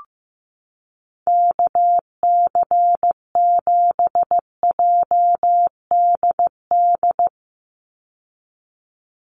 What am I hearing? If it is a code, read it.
Morse code, heard as a keyed tone, about 15 wpm, 700 Hz. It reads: KC7JDD
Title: USA Call Signs 15wpm